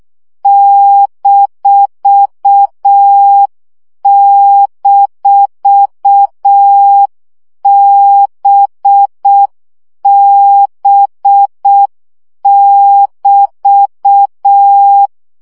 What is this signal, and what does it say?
Morse code, heard as a keyed tone, about 6 wpm, 800 Hz. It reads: --BB<BT>
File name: CW gyakorlas b 6 egyenlo minusz 5 baud